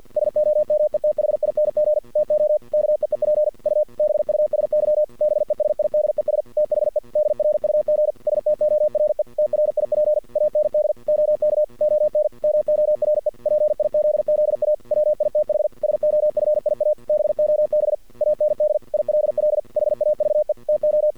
Morse code, heard as a keyed tone, about 50 wpm, 610 Hz. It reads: SOMETHING TO KEEP U BUSY BESIDES THE DRUG STORE TRIP AND OK ON MORE PIZZA WITH SOFIA BOB AND ILL HAVE TO
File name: archived / CRSnet